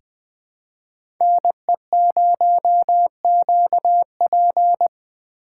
Morse code, heard as a keyed tone, about 20 wpm, 700 Hz. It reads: NE0QP
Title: USA Call Signs - 5 Characters - 4x Character Spacing - 20wpm